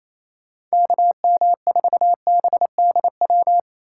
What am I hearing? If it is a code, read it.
Morse code, heard as a keyed tone, about 35 wpm, 700 Hz. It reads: KM4BDW